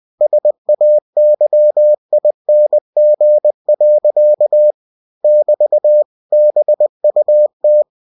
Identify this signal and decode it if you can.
Morse code, heard as a keyed tone, about 20 wpm, 600 Hz. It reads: SAYING. <BT> BUT
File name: winnie-20wpm